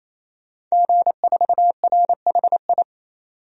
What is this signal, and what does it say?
Morse code, heard as a keyed tone, about 28 wpm, 700 Hz. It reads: G4RHI
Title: DXCC-100 Call Signs - 5 Characters - 28wpm